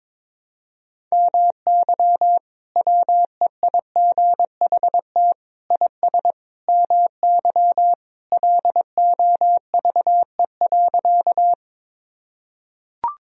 Morse code, heard as a keyed tone, about 22 wpm, 700 Hz. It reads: MY WEIGHT IS MY LOVE.
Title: Sentences from Top 1000 Words - Review Part 3 - 22wpm